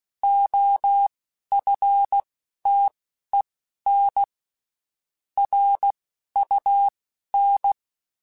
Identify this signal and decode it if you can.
Morse code, heard as a keyed tone, about 16 wpm, spaced as for 11 wpm, 800 Hz. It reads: OFTEN RUN
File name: (spaced) / QSOs Spaced-mp3s / WA7QOX 2 @15 sp